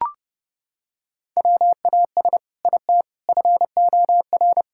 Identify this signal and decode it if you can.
Morse code, heard as a keyed tone, about 30 wpm, 700 Hz. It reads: WAS IT FOR
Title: Sentences from Top 1000 Words - Review Part 5 - 30wpm